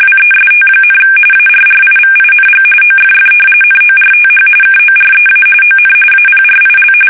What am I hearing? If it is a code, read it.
RTTY, about 45.45 baud. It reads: WEST, ARRIVING OFF NORTH KOREA ON APRIL
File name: fsk850 / battleshipnj